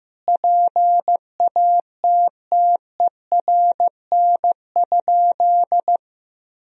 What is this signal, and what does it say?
Morse code, heard as a keyed tone, about 15 wpm, 700 Hz. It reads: PATTERN?
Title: Sentences from Top 1000 Words - Review Part 2 - 15wpm